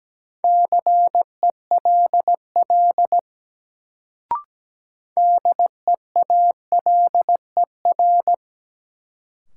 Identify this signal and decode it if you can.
Morse code, heard as a keyed tone, about 17 wpm, 700 Hz. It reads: CELL DEALER
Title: Sentences from Top 1100 Words - New Word Drill - 17wpm